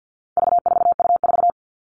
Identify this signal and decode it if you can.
Morse code, heard as a keyed tone, about 50 wpm, 700 Hz. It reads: V4U4
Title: All Cut Numbers - 4-Digit 50wpm